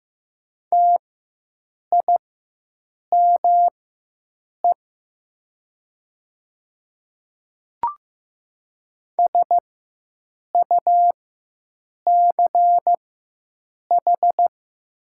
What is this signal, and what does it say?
Morse code, heard as a keyed tone, about 15 wpm, spaced as for 7 wpm, 700 Hz. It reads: TIME SUCH
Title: 4x Character Spacing - Top 200 Words - 15wpm